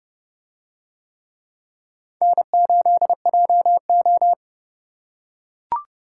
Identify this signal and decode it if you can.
Morse code, heard as a keyed tone, about 29 wpm, 700 Hz. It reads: N8JO